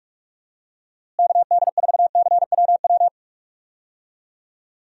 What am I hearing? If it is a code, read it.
Morse code, heard as a keyed tone, about 46 wpm, 700 Hz. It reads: KD4CWW
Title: USA Call Signs - 6 Characters - Repeated 3x - 45wpm